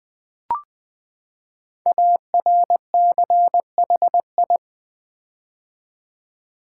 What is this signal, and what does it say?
Morse code, heard as a keyed tone, about 20 wpm, 700 Hz. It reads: ARCHI
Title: Build A Word - 5 Letters At A Time - 12-Letter Words - 20wpm